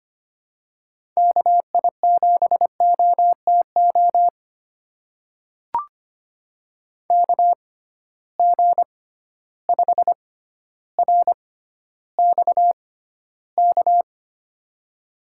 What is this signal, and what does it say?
Morse code, heard as a keyed tone, about 25 wpm, spaced as for 13 wpm, 700 Hz. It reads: KI7OTO KG5RXK
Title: USA Call Signs - 6 Characters - 6x Character Spacing - 25wpm